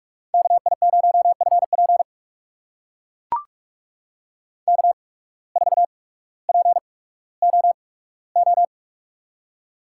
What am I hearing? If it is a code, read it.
Morse code, heard as a keyed tone, about 44 wpm, spaced as for 19 wpm, 700 Hz. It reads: KI0FP K4POO
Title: USA Call Signs - 5 Characters - 8x Character Spacing - 45wpm